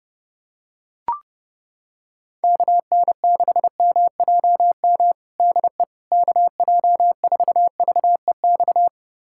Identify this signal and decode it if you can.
Morse code, heard as a keyed tone, about 30 wpm, 700 Hz. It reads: KN6MJM DE KJ4VEX
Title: USA Callsign DE Callsign - 6 Characters - 30wpm